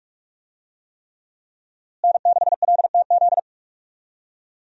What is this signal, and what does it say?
Morse code, heard as a keyed tone, about 41 wpm, 700 Hz. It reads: N6LTZ
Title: USA Call Signs - 5 Characters - Repeated 3x - 45wpm